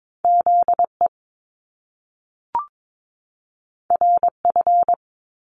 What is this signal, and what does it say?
Morse code, heard as a keyed tone, about 22 wpm, 700 Hz. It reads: ZE RF
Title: Top Two Letter Combinations in Words 22wpm